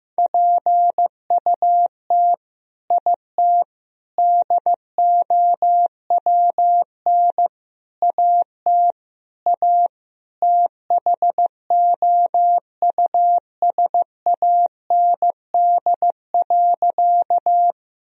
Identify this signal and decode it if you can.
Morse code, heard as a keyed tone, about 15 wpm, 700 Hz. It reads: PUT IT DOWN AT A THOUSAND.
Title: Sentences from Top 1000 Words - Review Part 3 - 15wpm